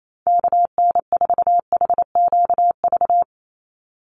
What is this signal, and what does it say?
Morse code, heard as a keyed tone, about 28 wpm, 700 Hz. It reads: KN4HQV
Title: USA Call Signs 28wpm